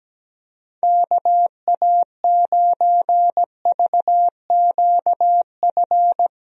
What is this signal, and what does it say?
Morse code, heard as a keyed tone, about 17 wpm, 700 Hz. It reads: KA9VQF
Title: USA Call Signs - 6 Characters - Repeated 2x - 17wpm